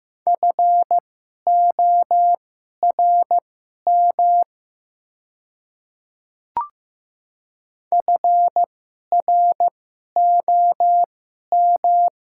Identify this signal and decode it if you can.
Morse code, heard as a keyed tone, about 15 wpm, spaced as for 11 wpm, 700 Hz. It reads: FORM FROM
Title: 2x Character Spacing - Top 200 Words - 15wpm